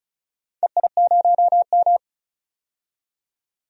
Morse code, heard as a keyed tone, about 34 wpm, 700 Hz. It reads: EI0M
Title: DXCC-100 Call Signs - 4 Characters - 35wpm